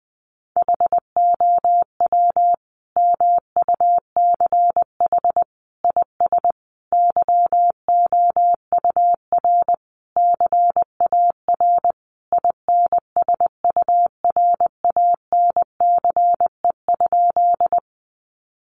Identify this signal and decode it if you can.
Morse code, heard as a keyed tone, about 20 wpm, 700 Hz. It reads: HOW MUCH IS YOUR CAR INSURANCE?
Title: Sentences from Top 1100 Words 20wpm